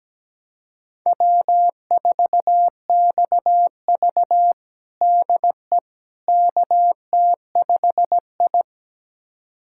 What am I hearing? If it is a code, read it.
Morse code, heard as a keyed tone, about 17 wpm, 700 Hz. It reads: W4XV DE KT5I
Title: USA Callsign DE Callsign - 4 Characters - 17wpm